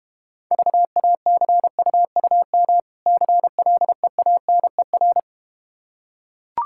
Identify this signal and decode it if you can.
Morse code, heard as a keyed tone, about 31 wpm, 700 Hz. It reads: VACUUM CLEANER
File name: Easy Words - Speed-Racing with Hints - 25wpm